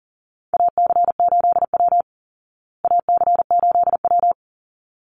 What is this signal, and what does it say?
Morse code, heard as a keyed tone, about 40 wpm, 700 Hz. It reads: AC8W AC8W
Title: Call Signs Repeated 3x - 40wpm